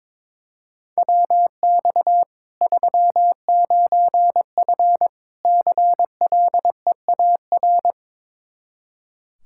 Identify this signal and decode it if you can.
Morse code, heard as a keyed tone, about 22 wpm, 700 Hz. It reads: WX 39F CLEAR
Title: Practice QSO - Exchange 2 - Info - 22wpm